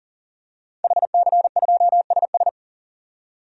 Morse code, heard as a keyed tone, about 37 wpm, 700 Hz. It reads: HC2SS
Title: DXCC-100 Call Signs - 5 Characters - 40wpm